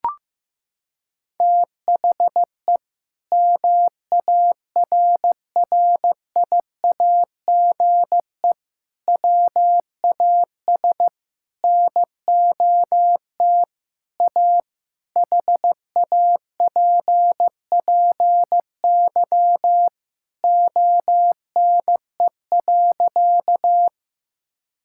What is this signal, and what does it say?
Morse code, heard as a keyed tone, about 15 wpm, 700 Hz. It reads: THE MARRIAGE WAS NOT A HAPPY ONE.
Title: Sentences from Top 1000 Words - Review Part 5 - 15wpm